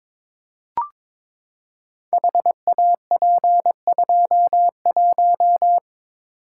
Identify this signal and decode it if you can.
Morse code, heard as a keyed tone, about 22 wpm, 700 Hz. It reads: HAP21
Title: Code Groups - 5 Letter:Numbers - 22wpm